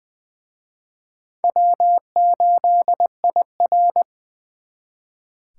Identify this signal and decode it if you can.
Morse code, heard as a keyed tone, about 24 wpm, 700 Hz. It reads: W8IR
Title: Speed Racing - USA 1x2 Call Signs 20wpm